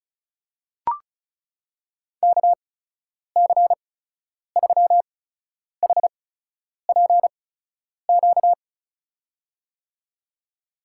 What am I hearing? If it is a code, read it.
Morse code, heard as a keyed tone, about 36 wpm, spaced as for 10 wpm, 700 Hz. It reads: KC3HPQ
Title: USA Call Signs - 8x Character Spacing - 35wpm